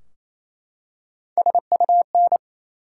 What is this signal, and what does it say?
Morse code, heard as a keyed tone, about 28 wpm, 700 Hz. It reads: SUN
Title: Three Letter Words - 28wpm